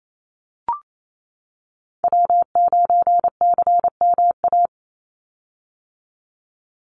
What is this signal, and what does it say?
Morse code, heard as a keyed tone, about 28 wpm, 700 Hz. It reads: W9CMA
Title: USA Call Signs 28wpm